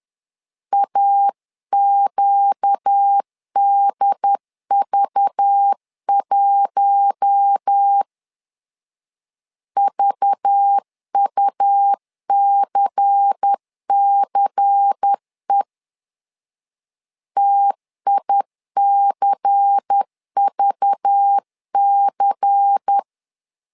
Morse code, heard as a keyed tone, about 10 wpm, 800 Hz. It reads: AQDV1 VUCCE TICVC
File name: LEZIONI / 5 LBVC / 5 LBVC 10wpm 2349